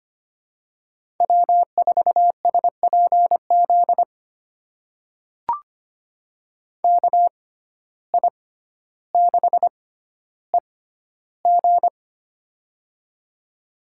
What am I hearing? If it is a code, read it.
Morse code, heard as a keyed tone, about 25 wpm, spaced as for 13 wpm, 700 Hz. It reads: W4SPZ KI6EG
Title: USA Call Signs - 5 Characters - 6x Character Spacing - 25wpm